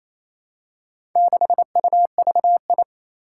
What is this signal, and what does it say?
Morse code, heard as a keyed tone, about 27 wpm, 700 Hz. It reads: 6UVI
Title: Four Challenging Letter-Number Combinations 28wpm